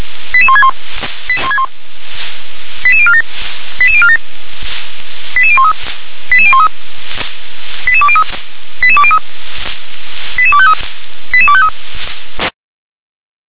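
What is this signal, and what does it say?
Five-tone selective call (ZVEI), heard as quick runs of five tones, tones of about 70 ms. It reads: 88161 88047 88047 88013 88013 88203 88203 88253 88253
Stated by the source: ZVEI five-tone call received by SDR.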